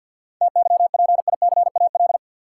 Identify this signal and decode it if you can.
Morse code, heard as a keyed tone, about 53 wpm, 700 Hz. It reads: TYPICAL
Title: Sentences from Top 1100 Words - New Word Drill Alt - 50wpm